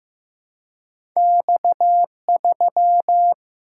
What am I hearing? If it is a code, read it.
Morse code, heard as a keyed tone, about 15 wpm, 700 Hz. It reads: X3
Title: Single-Letter Single-Number 15wpm